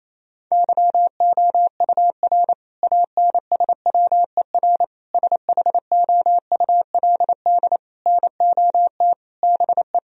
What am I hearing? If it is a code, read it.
Morse code, heard as a keyed tone, about 28 wpm, 700 Hz. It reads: YOUR ANSWER SHOULD NOT BE
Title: Sentences from Top 1000 Words - Review Part 3 - 28wpm